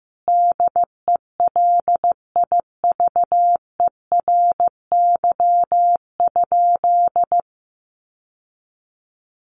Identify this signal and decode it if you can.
Morse code, heard as a keyed tone, about 15 wpm, 700 Hz. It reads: DELIVERY?
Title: Sentences from Top 1500 Words 15wpm